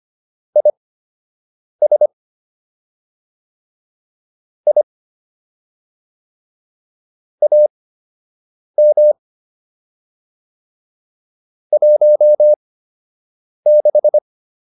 Morse code, heard as a keyed tone, about 25 wpm, spaced as for 7 wpm, 600 Hz. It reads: IS I AM 16